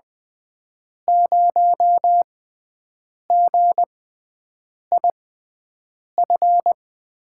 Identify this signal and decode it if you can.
Morse code, heard as a keyed tone, about 20 wpm, spaced as for 7 wpm, 700 Hz. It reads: 0GIF
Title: USA Call Signs - 6 Characters - 6x Character Spacing - 20wpm